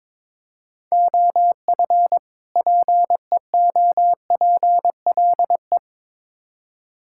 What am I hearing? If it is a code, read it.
Morse code, heard as a keyed tone, about 21 wpm, 700 Hz. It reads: OF PEOPLE
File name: Build A Phrase - 2-Word Phrases - 22wpm